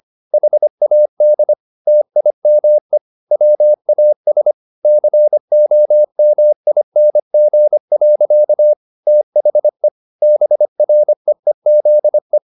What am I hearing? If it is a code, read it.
Morse code, heard as a keyed tone, about 25 wpm, 600 Hz. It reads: HAD TIME WAS COMING. THE BREEZE